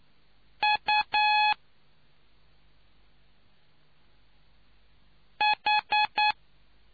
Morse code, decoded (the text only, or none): UH